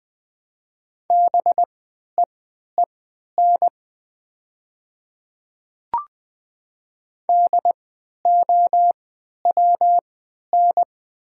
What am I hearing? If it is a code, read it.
Morse code, heard as a keyed tone, about 20 wpm, spaced as for 11 wpm, 700 Hz. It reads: BEEN DOWN
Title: 3x Character Spacing - Top 200 Words - 20wpm